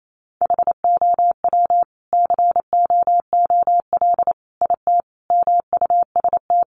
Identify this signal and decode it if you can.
Morse code, heard as a keyed tone, about 28 wpm, 700 Hz. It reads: HOW COOL IT MUST
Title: Sentences from Top 1200 Words 28wpm